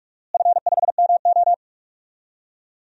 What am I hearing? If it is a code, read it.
Morse code, heard as a keyed tone, about 42 wpm, 700 Hz. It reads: F5MO
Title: DXCC-100 Call Signs - 4 Characters - Repeated 2x - 45wpm